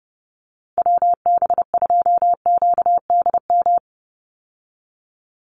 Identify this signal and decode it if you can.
Morse code, heard as a keyed tone, about 30 wpm, 700 Hz. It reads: WB2QDM
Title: USA Call Signs 30wpm